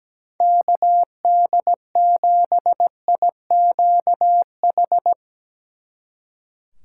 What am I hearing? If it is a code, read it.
Morse code, heard as a keyed tone, about 17 wpm, 700 Hz. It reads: KD7IQH
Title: Speed Racing - USA 2x3 Call Signs 17wpm